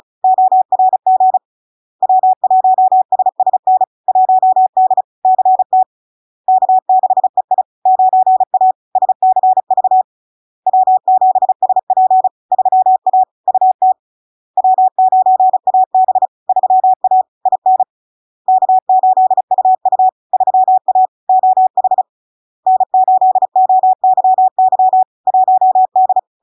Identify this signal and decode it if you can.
Morse code, heard as a keyed tone, about 35 wpm, 750 Hz. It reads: ORG W1SSN 1D CT K6EI 9A SCV W7SP 3A UT W9AB 3A IN K8UU 3A OH N8OYY 1D
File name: LARK-FD-2022-Log-in-CW-35wpm-750-Hz